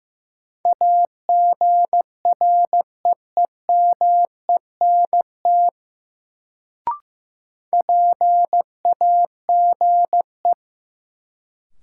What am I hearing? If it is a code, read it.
Morse code, heard as a keyed tone, about 15 wpm, 700 Hz. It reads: AGREEMENT PAGE
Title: Sets of 5 Words Encore - New Word Drill - 15wpm